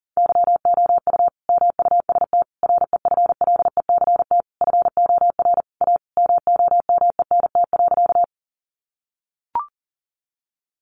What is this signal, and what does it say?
Morse code, heard as a keyed tone, about 37 wpm, 700 Hz. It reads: YOU MUST REFLECT FOR A MOMENT.
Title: Sentences from Top 1500 Words 40wpm